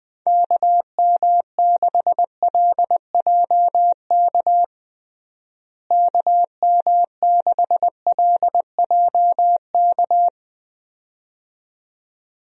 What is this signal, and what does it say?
Morse code, heard as a keyed tone, about 21 wpm, 700 Hz. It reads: KM6LJK KM6LJK
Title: USA Call Signs - 6 Characters - Repeated 2x - 20wpm